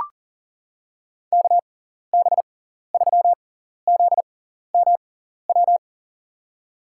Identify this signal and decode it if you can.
Morse code, heard as a keyed tone, about 40 wpm, spaced as for 14 wpm, 700 Hz. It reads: KB3ZMW
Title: USA Call Signs - 6 Characters - 6x Character Spacing - 40wpm